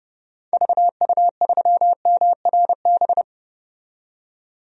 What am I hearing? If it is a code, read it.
Morse code, heard as a keyed tone, about 30 wpm, 700 Hz. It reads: VU3MRB
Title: DXCC-100 Call Signs - 6 Characters - 30wpm